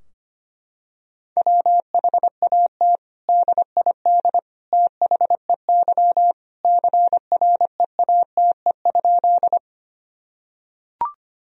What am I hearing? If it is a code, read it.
Morse code, heard as a keyed tone, about 25 wpm, 700 Hz. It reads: WHAT DID THEY CREATE?
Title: Sentences from Top 1000 Words - Review Part 3 - 25wpm